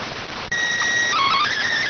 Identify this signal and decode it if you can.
SSTV (slow-scan television), PD 290.